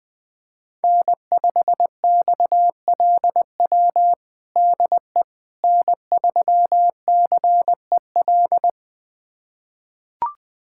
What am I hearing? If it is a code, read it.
Morse code, heard as a keyed tone, about 20 wpm, 700 Hz. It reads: N5XLW DE N3CEL
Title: USA Callsign DE Callsign 20wpm